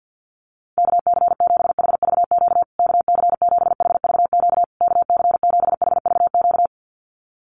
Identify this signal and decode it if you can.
Morse code, heard as a keyed tone, about 48 wpm, 700 Hz. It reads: KC7HVQ KC7HVQ KC7HVQ
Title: USA Call Signs Repeated 3x 50wpm